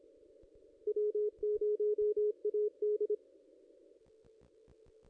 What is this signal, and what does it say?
Morse code, heard as a keyed tone, about 29 wpm, 440 Hz. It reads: W0AD
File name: contests / ss-cw-k5tr-op-n5rz -1700